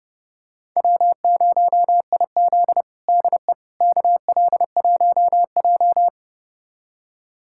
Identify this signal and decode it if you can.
Morse code, heard as a keyed tone, about 30 wpm, 700 Hz. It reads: W0IZ DE KL1J
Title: USA Callsign DE Callsign - 4 Characters - 30wpm